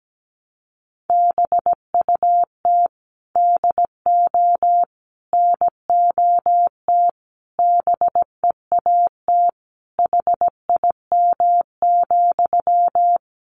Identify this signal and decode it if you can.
Morse code, heard as a keyed tone, about 17 wpm, 700 Hz. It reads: BUT DO NOT BEAT HIM,
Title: Sentences from Top 1300 Words 17wpm